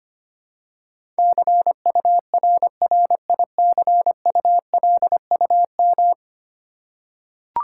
Morse code, heard as a keyed tone, about 25 wpm, 690 Hz. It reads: CURRICULUM
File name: 10 Letter Words - First 3 Letters Given - 25wpm